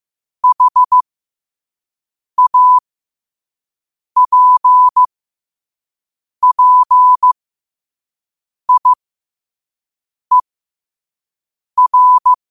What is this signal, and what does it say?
Morse code, heard as a keyed tone, about 15 wpm, spaced as for 5 wpm, 1000 Hz. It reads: HAPPIER
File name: qotdfiles / 5wpm